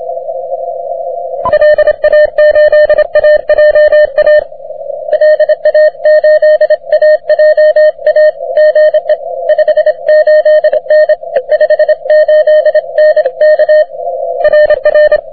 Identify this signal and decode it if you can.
Morse code, heard as a keyed tone, about 26 wpm, 610 Hz. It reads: LA8AJA LA8AJA GE 58N 58N K RR
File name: wp / vq9jc 30m cw